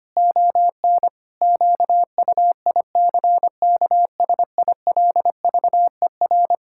Morse code, heard as a keyed tone, about 25 wpm, 700 Hz. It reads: ON QUICKSILVER
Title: Pangram Challenge v2 - 25wpm